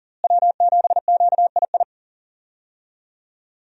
Morse code, heard as a keyed tone, about 40 wpm, 700 Hz. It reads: W7QII
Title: USA Call Signs - 5 Characters - Repeated 2x - 40wpm